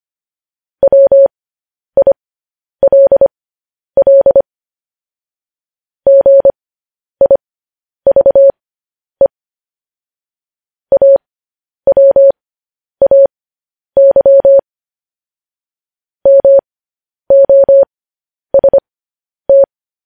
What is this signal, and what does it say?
Morse code, heard as a keyed tone, about 25 wpm, spaced as for 10 wpm, 550 Hz. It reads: WILL GIVE AWAY MOST
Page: News Headlines in Morse Code at 10 WPM